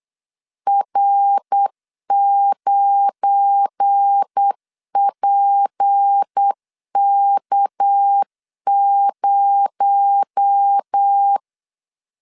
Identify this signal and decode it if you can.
Morse code, heard as a keyed tone, about 8 wpm, 800 Hz. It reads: R9PK0
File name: LEZIONI / 7 GF9Z / 7 GF9 8wpm 3545